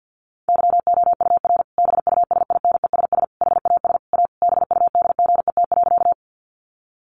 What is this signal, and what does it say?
Morse code, heard as a keyed tone, about 50 wpm, 700 Hz. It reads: YOUR BUSINESS HAS A BUDGET.
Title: Sentences from Top 1200 Words 50wpm